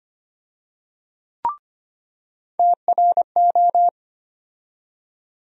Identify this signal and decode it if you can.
Morse code, heard as a keyed tone, about 25 wpm, 700 Hz. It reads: TRO
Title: USA Call Sign Postfixes - 3 Letters 25wpm